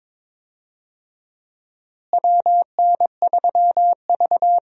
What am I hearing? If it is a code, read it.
Morse code, heard as a keyed tone, about 22 wpm, 700 Hz. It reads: WN3V
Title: USA Call Signs - 4 Characters - Repeated 3x - 22wpm